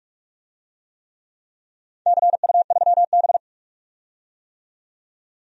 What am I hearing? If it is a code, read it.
Morse code, heard as a keyed tone, about 46 wpm, 700 Hz. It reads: CU3B
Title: DXCC-100 Call Signs - 4 Characters - 45wpm